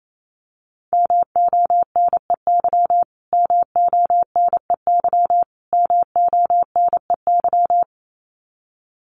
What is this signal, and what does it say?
Morse code, heard as a keyed tone, about 28 wpm, 700 Hz. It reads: MONEY MONEY MONEY
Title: Top 500 Words Repeated 28wpm